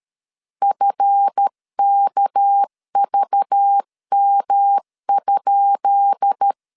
Morse code, heard as a keyed tone, about 13 wpm, 800 Hz. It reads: FKVM?
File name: LEZIONI / 10 SIGNS / 10B INT 12wpm 2414